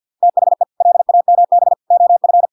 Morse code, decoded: THE LAMB OF

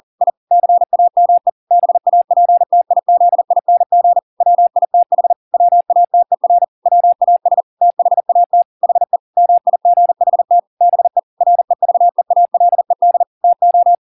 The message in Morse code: I CAME BAPTIZING WITH WATER WAS THAT HE MIGHT BE REVEALED TO

